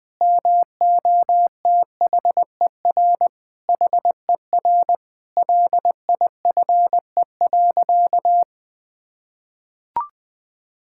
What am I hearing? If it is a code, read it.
Morse code, heard as a keyed tone, about 21 wpm, 700 Hz. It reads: MOTHER HER LIFE.
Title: Sentences from Top 1000 Words - Review Part 5 - 20wpm